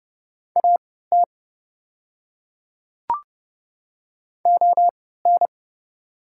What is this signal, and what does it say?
Morse code, heard as a keyed tone, about 29 wpm, spaced as for 17 wpm, 690 Hz. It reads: AT ON
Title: 3x Character Spacing - Top 100 Words - 30wpm